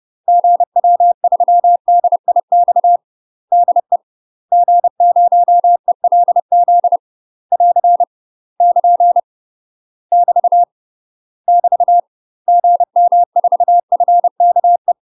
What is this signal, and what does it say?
Morse code, heard as a keyed tone, about 30 wpm, 700 Hz. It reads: GW3DIX DE G0ELZ <AR> <KN> <BT> <BT> GM4FKE
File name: UK MASTER QSOs TRAINING / 30WPM / UK MASTER QSOs 0019